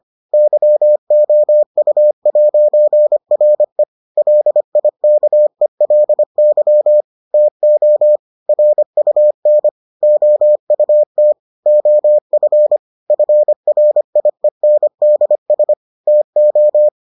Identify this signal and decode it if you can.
Morse code, heard as a keyed tone, about 25 wpm, 600 Hz. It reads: YOU'RE LIKELY TO RUN OUT OF FRIENDS TO